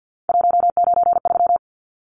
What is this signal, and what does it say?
Morse code, heard as a keyed tone, about 50 wpm, 700 Hz. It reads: 193
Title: Numbers - 3-Digit 50wpm